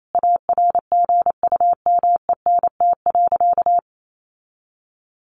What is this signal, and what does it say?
Morse code, heard as a keyed tone, about 28 wpm, 700 Hz. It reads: ARGUMENT.
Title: Sentences from Top 1400 Words 28wpm